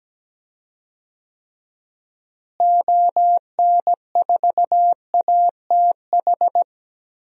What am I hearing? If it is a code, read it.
Morse code, heard as a keyed tone, about 17 wpm, 700 Hz. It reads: ON4ATH
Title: DXCC-100 Call Signs - 6 Characters - Repeated 2x - 17wpm